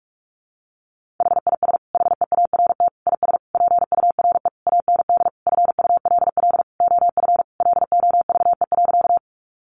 Morse code, heard as a keyed tone, about 44 wpm, 700 Hz. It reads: HIS HEART IS PURE AND FULL OF LOVE.
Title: Sentences from Top 1500 Words 45wpm